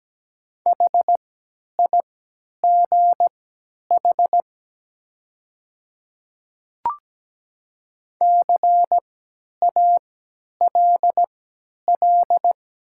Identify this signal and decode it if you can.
Morse code, heard as a keyed tone, about 17 wpm, spaced as for 10 wpm, 690 Hz. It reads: HIGH CALL
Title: 3x Character Spacing - Top 200 Words - 17wpm